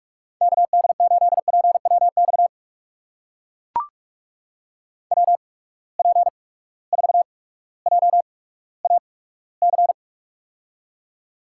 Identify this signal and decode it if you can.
Morse code, heard as a keyed tone, about 44 wpm, spaced as for 19 wpm, 700 Hz. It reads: KD8PWX WP4JAC
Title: USA Call Signs - 6 Characters - 8x Character Spacing - 45wpm